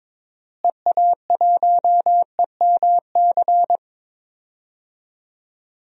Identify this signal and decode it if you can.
Morse code, heard as a keyed tone, about 22 wpm, 700 Hz. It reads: EA1EMC
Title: DXCC-100 Call Signs - 6 Characters - Repeated 3x - 22wpm